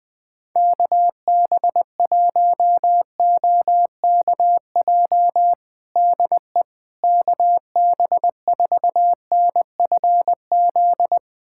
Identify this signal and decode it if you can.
Morse code, heard as a keyed tone, about 21 wpm, 700 Hz. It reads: KB1OKJ DE KB4NFZ
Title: USA Callsign DE Callsign 20wpm